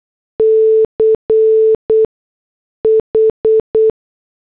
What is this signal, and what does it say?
Morse code, heard as a keyed tone, about 8 wpm, spaced as for 6 wpm, 440 Hz. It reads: CH